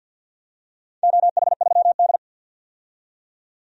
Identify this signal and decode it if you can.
Morse code, heard as a keyed tone, about 53 wpm, 700 Hz. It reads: OH3D
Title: DXCC-100 Call Signs - 4 Characters - 50wpm